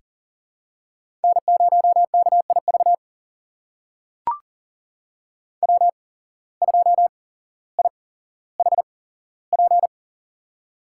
Morse code, heard as a keyed tone, about 38 wpm, spaced as for 17 wpm, 700 Hz. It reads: N0KIV W2IHP
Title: USA Call Signs - 5 Characters - 8x Character Spacing - 40wpm